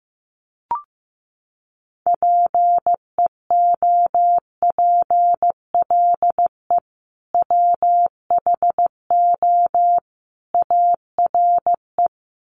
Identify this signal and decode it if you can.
Morse code, heard as a keyed tone, about 15 wpm, 700 Hz. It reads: PEOPLE WHO ARE
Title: Sets of 3 Words 15wpm